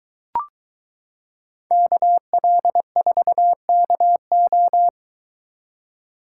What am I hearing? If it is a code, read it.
Morse code, heard as a keyed tone, about 22 wpm, 700 Hz. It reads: KL4KO